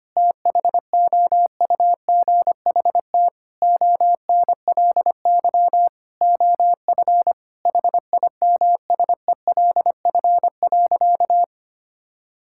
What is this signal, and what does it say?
Morse code, heard as a keyed tone, about 25 wpm, 700 Hz. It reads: THOUGHT ONLY OF HIMSELF.
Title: Sentences from Top 1600 Words 25wpm